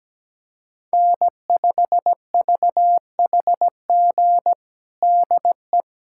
Morse code, heard as a keyed tone, about 17 wpm, 700 Hz. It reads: N5VHG DE
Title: USA Callsign DE Callsign - 5 Characters - 17wpm